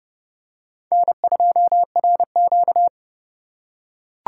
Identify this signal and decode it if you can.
Morse code, heard as a keyed tone, about 30 wpm, 700 Hz. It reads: N2RQ